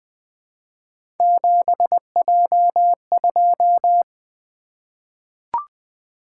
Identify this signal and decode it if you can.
Morse code, heard as a keyed tone, about 20 wpm, 700 Hz. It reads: 7J2
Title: Three Letter-Number Combinations - 20wpm